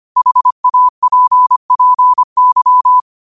Morse code, heard as a keyed tone, about 25 wpm, 1000 Hz. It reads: HAPPY
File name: qotdfiles / 25wpm